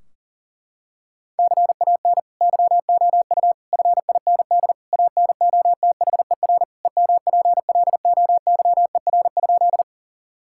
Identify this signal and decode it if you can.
Morse code, heard as a keyed tone, about 40 wpm, 700 Hz. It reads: CAN YOU FIND ANOTHER EMPLOYER?
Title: Sentences from Top 1600 Words 40wpm